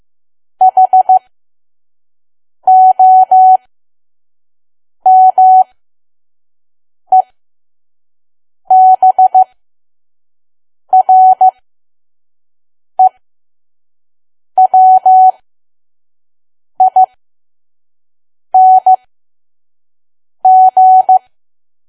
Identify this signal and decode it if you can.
Morse code, heard as a keyed tone, about 15 wpm, spaced as for 5 wpm, 750 Hz. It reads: HOMEBREWING
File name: Archive / 5 WPM files / PM